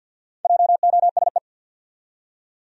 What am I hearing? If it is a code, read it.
Morse code, heard as a keyed tone, about 53 wpm, 700 Hz. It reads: JOSE
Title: USA Common Names - Max 4 Letters - 50wpm